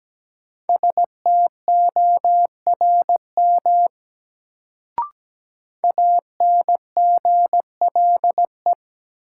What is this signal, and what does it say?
Morse code, heard as a keyed tone, about 17 wpm, 700 Hz. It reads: STORM ANGLE
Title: Sentences from Top 1500 Words - New Word Drill - 17wpm